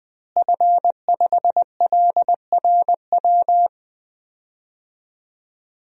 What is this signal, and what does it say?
Morse code, heard as a keyed tone, about 20 wpm, 700 Hz. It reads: F5LRW
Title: DXCC-100 Call Signs - 5 Characters - Repeated 2x - 20wpm